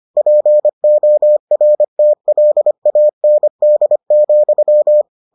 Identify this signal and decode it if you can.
Morse code, heard as a keyed tone, about 25 wpm, 600 Hz. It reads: PORTLAND,